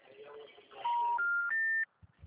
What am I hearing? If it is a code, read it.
Special information tone.